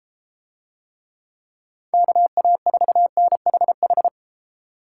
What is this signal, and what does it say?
Morse code, heard as a keyed tone, about 32 wpm, 700 Hz. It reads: KA4NHH